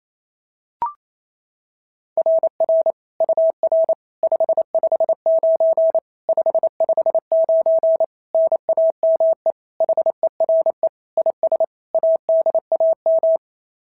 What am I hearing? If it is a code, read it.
Morse code, heard as a keyed tone, about 27 wpm, 650 Hz. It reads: RR UR 559 559 NAME HERE IS ADAM
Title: SOTA QSO Practice 28wpm